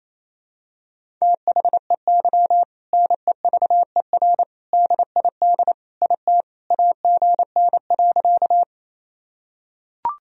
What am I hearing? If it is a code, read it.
Morse code, heard as a keyed tone, about 28 wpm, 700 Hz. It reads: THEY NEVER DID IT AGN.
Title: Sentences from Top 300 Words and Abbreviations 28wpm